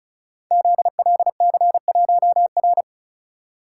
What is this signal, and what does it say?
Morse code, heard as a keyed tone, about 36 wpm, 690 Hz. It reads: ZLC1R
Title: Code Groups - 5 Letter:Numbers - 35wpm